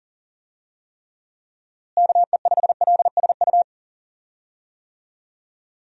Morse code, heard as a keyed tone, about 40 wpm, 700 Hz. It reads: KE5LSU
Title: USA Call Signs - 6 Characters - Repeated 2x - 40wpm